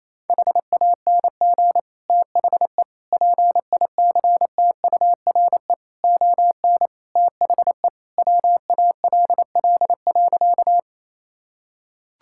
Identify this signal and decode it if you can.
Morse code, heard as a keyed tone, about 28 wpm, 700 Hz. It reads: HANG THE PICTURE ON THE WALL.
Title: Sentences from Top 1600 Words 28wpm